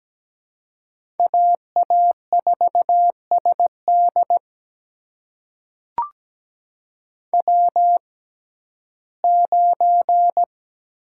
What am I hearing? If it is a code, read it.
Morse code, heard as a keyed tone, about 17 wpm, spaced as for 12 wpm, 700 Hz. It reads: AA4SD W9
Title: USA Call Signs - 5 Characters - 6x Character Spacing - 17wpm